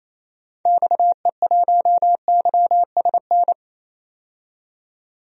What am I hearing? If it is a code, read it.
Morse code, heard as a keyed tone, about 28 wpm, 700 Hz. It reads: XE1YSN
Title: DXCC-100 Call Signs - 6 Characters - 28wpm